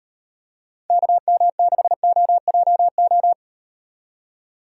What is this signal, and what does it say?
Morse code, heard as a keyed tone, about 38 wpm, 700 Hz. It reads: KM6OJO